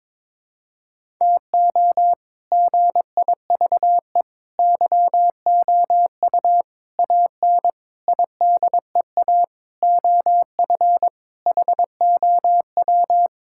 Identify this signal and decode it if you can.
Morse code, heard as a keyed tone, about 22 wpm, 700 Hz. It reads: TO GIVE YOU AN IDEA OF HOW
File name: practice / 22wpm / Sets of 7 Words - 22wpm